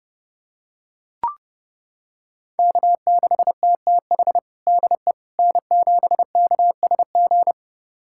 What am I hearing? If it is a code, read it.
Morse code, heard as a keyed tone, about 30 wpm, 700 Hz. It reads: K6TTH DE N7KSG
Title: USA Callsign DE Callsign - 5 Characters - 30wpm